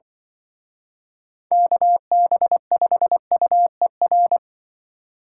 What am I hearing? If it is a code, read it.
Morse code, heard as a keyed tone, about 23 wpm, 700 Hz. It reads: KB5UER
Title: Speed Racing - USA 2x3 Call Signs 20wpm